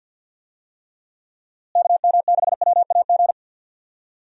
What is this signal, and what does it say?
Morse code, heard as a keyed tone, about 51 wpm, 700 Hz. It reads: KM6WAG
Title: USA Call Signs - 6 Characters - Repeated 3x - 50wpm